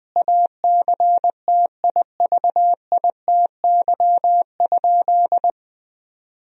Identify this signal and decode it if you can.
Morse code, heard as a keyed tone, about 20 wpm, 700 Hz. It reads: ACTIVITY?
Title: Sentences from Top 1000 Words - Review Part 4 - 20wpm